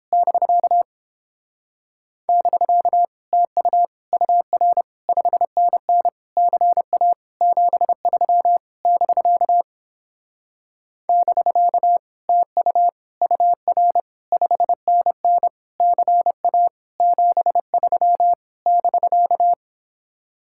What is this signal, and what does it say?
Morse code, heard as a keyed tone, about 31 wpm, 700 Hz. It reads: <BK> <BK> TU UR 5NN CA 73 <BK> <BK> TU UR 5NN CA 73 <BK>